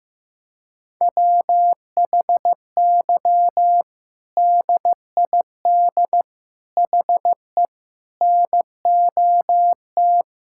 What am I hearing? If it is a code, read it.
Morse code, heard as a keyed tone, about 16 wpm, 700 Hz. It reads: WHY DID HE NOT
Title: Sentences from Top 1600 Words 15wpm